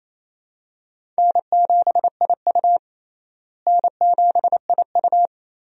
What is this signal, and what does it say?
Morse code, heard as a keyed tone, about 28 wpm, 700 Hz. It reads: N7IU N7IU
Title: USA Call Signs - 4 Characters - Repeated 3x - 28wpm